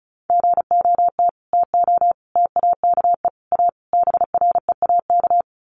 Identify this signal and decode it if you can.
Morse code, heard as a keyed tone, about 34 wpm, 700 Hz. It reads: GOT TO TAKE A BREAK
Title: Sets of 5 Words Encore 35wpm